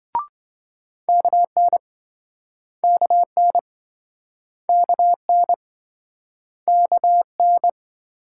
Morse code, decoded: KN KN KN KN

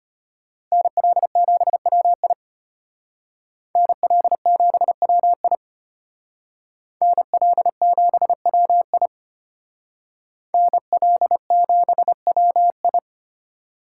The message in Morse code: NL7WI NL7WI NL7WI NL7WI

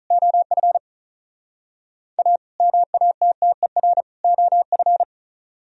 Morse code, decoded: OF A MATTER OF

